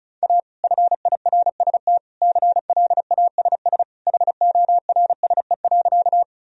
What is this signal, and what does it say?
Morse code, heard as a keyed tone, about 36 wpm, 700 Hz. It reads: A FIRST CLASS HORSE.
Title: Sentences from Top 1000 Words - Review Part 2 - 35wpm